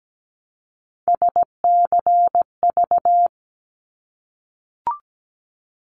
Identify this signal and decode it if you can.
Morse code, heard as a keyed tone, about 17 wpm, 700 Hz. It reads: SCV
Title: ARRL Sections 17wpm